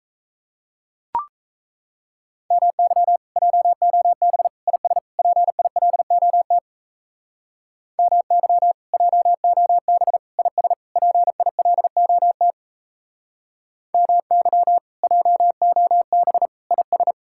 Morse code, heard as a keyed tone, about 35 wpm, 700 Hz. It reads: MY JOB IS PILOT MY JOB IS PILOT MY JOB IS